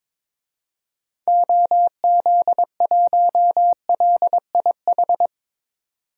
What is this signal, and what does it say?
Morse code, heard as a keyed tone, about 22 wpm, 700 Hz. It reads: OZ1LIH
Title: DXCC-100 Call Signs - 6 Characters - 22wpm